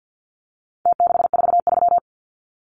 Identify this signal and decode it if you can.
Morse code, heard as a keyed tone, about 49 wpm, 700 Hz. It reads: T643
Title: Numbers - Contest Style - 4-Digit 50wpm